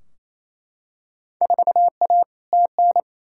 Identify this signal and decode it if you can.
Morse code, heard as a keyed tone, about 28 wpm, 700 Hz. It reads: 4A TN
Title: Field Day Exchanges Encore - 28wpm